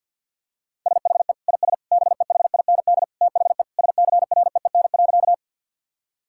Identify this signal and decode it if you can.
Morse code, heard as a keyed tone, about 48 wpm, 700 Hz. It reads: SHE IS BEHIND THE SCREEN.
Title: Sentences from Top 1000 Words - Review Part 6 - 50wpm